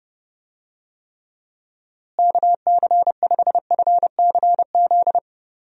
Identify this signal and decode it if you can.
Morse code, heard as a keyed tone, about 30 wpm, 700 Hz. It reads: KC5FCZ
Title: USA Call Signs - 6 Characters - Repeated 2x - 30wpm